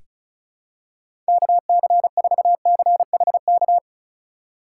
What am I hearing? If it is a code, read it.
Morse code, heard as a keyed tone, about 34 wpm, 700 Hz. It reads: KC4CHK
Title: USA Call Signs - 6 Characters - 4x Character Spacing - 35wpm